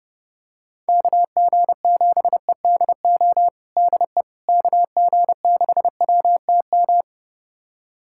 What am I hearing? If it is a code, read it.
Morse code, heard as a keyed tone, about 30 wpm, 700 Hz. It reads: KG7EDO DE KG6WTM
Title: USA Callsign DE Callsign 30wpm